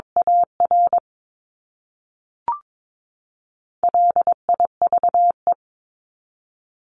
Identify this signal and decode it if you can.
Morse code, heard as a keyed tone, about 22 wpm, 700 Hz. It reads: AR LIVE
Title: ES:S Suffix - Top 100 Words with Base Words 22wpm